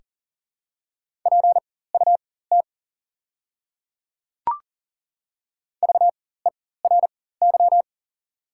Morse code, heard as a keyed tone, about 38 wpm, spaced as for 18 wpm, 700 Hz. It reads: PUT VERY
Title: 4x Character Spacing - Top 200 Words - 40wpm